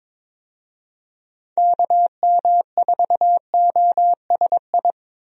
Morse code, heard as a keyed tone, about 22 wpm, 700 Hz. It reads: KM4OSI
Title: USA Call Signs - 6 Characters - 22wpm